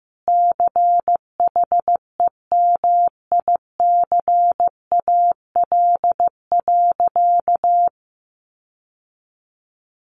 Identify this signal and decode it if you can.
Morse code, heard as a keyed tone, about 15 wpm, 700 Hz. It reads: CHEMICAL.
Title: Sentences from Top 1200 Words 15wpm